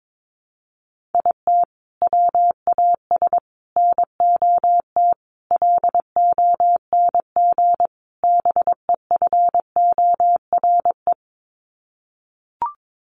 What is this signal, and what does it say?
Morse code, heard as a keyed tone, about 22 wpm, 700 Hz. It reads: IT WAS NOT LONG BEFORE
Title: Sets of 5 Words Encore2 22wpm